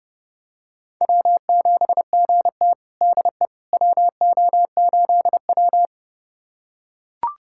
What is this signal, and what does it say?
Morse code, heard as a keyed tone, about 30 wpm, 700 Hz. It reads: W7GT DE WO8W
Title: USA Callsign DE Callsign - 4 Characters - 30wpm